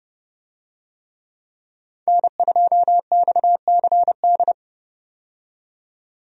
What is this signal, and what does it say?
Morse code, heard as a keyed tone, about 31 wpm, 700 Hz. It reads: N2XCD
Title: USA Call Signs - 5 Characters - 30wpm